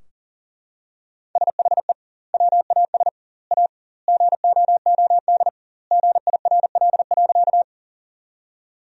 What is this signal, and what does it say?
Morse code, heard as a keyed tone, about 40 wpm, 700 Hz. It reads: SHE WAS A GOOD GIRL.
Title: 2x Word-Spacing - Sentences from Top 400 Words - 40wpm